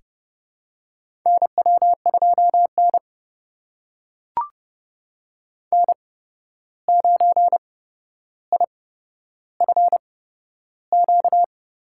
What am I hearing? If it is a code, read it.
Morse code, heard as a keyed tone, about 30 wpm, spaced as for 12 wpm, 700 Hz. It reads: NW2N N9IFQ
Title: USA Call Signs - 8x Character Spacing - 30wpm